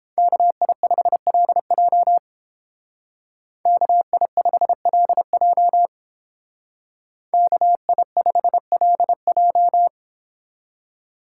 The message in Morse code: KI5LJ KI5LJ KI5LJ

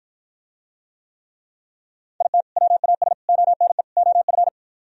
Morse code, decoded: IT WAS ONE OF

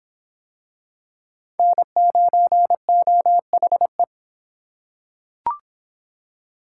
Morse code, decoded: N9OHE

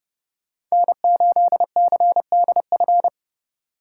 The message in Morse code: N8CDF

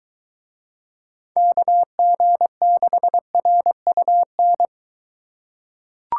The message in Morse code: KG6RUN

